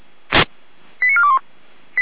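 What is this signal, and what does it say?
Five-tone selective call (ZVEI), heard as quick runs of five tones, tones of about 70 ms. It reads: 89421
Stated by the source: Five-tone ZVEI selective call received on SDR.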